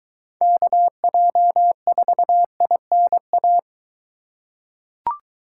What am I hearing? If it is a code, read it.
Morse code, heard as a keyed tone, about 22 wpm, 700 Hz. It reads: KJ4INA